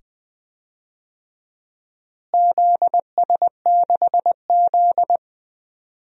Morse code, heard as a keyed tone, about 21 wpm, 700 Hz. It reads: ZS6Z
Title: DXCC-100 Call Signs - 4 Characters - Repeated 2x - 20wpm